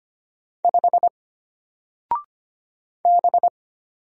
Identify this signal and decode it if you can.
Morse code, heard as a keyed tone, about 25 wpm, 700 Hz. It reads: 5 B
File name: Single-Letter-Number-25wpm